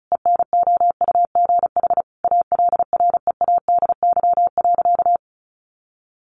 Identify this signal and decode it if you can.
Morse code, heard as a keyed tone, about 34 wpm, 700 Hz. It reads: ENOUGH ALREADY.
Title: Sentences from Top 700 Words 35wpm